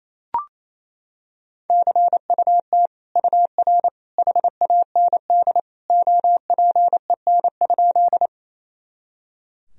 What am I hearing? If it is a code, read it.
Morse code, heard as a keyed tone, about 28 wpm, 700 Hz. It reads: CUT UR HAND OPEN?
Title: Sentences from Top 300 Words and Abbreviations 28wpm